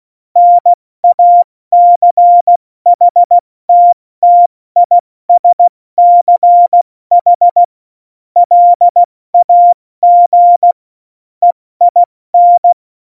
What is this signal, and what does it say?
Morse code, heard as a keyed tone, about 16 wpm, spaced as for 14 wpm, 700 Hz. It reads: NACHTTISCH LAG EIN
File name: Morsekurs / Witze / Witz 01. September 2020, 70 F80